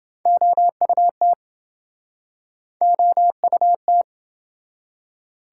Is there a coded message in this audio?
Yes: Morse code from 0.3 to 4.0 s, heard as a keyed tone, about 28 wpm, 700 Hz. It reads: OUT OUT